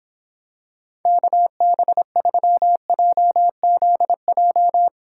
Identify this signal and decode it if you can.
Morse code, heard as a keyed tone, about 26 wpm, 700 Hz. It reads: KB3JZJ